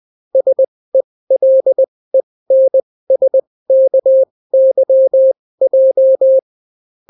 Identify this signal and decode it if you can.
Morse code, heard as a keyed tone, about 20 wpm, spaced as for 16 wpm, 530 Hz. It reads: SELENSKYJ